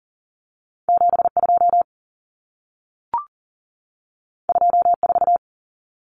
Morse code, heard as a keyed tone, about 40 wpm, 700 Hz. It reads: 72 24
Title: Common Cut Numbers - 2-Digit 40wpm